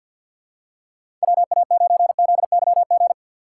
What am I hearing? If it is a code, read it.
Morse code, heard as a keyed tone, about 53 wpm, 700 Hz. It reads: WA9ZYG
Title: USA Call Signs - 6 Characters - 4x Character Spacing - 50wpm